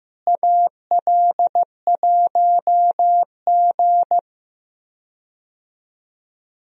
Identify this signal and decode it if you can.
Morse code, heard as a keyed tone, about 15 wpm, 700 Hz. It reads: AL1G
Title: USA Call Signs - 4 Characters - Repeated 2x - 15wpm